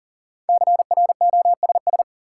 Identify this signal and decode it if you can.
Morse code, heard as a keyed tone, about 40 wpm, 700 Hz. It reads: CROSS
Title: Sentences from Top 300 Words - New Word Drill - 40wpm